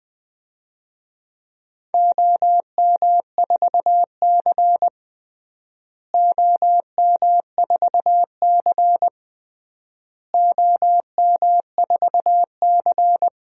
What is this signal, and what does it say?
Morse code, heard as a keyed tone, about 20 wpm, 700 Hz. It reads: OM4C OM4C OM4C
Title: DXCC-100 Call Signs - 4 Characters - Repeated 3x - 20wpm